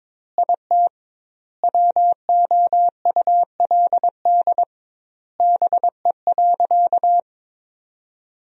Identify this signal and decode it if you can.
Morse code, heard as a keyed tone, about 22 wpm, 700 Hz. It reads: IT WOULD BE.
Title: 2x Word-Spacing - Sentences from Top 200 Words - 22wpm